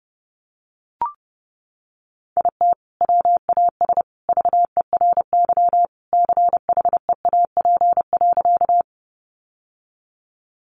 Morse code, heard as a keyed tone, about 30 wpm, 700 Hz. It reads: IT WAS VERY CHEAP.
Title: Sentences from Top 1200 Words 30wpm